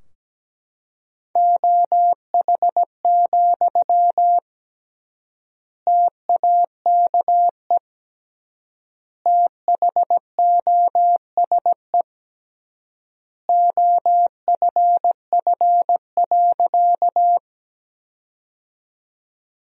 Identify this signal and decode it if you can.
Morse code, heard as a keyed tone, about 17 wpm, 700 Hz. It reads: OH, TAKE THOSE OFF.
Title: 3x Word-Spacing - Sentences from Top 500 Words - 17wpm